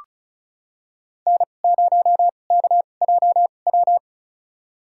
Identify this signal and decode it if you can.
Morse code, heard as a keyed tone, about 36 wpm, spaced as for 25 wpm, 700 Hz. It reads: N0KJW
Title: USA Call Signs - 5 Characters - 2x Character Spacing - 35wpm